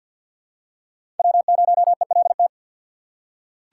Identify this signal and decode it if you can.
Morse code, heard as a keyed tone, about 47 wpm, 700 Hz. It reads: W0ELT
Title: USA Call Signs - 5 Characters - 2x Character Spacing - 50wpm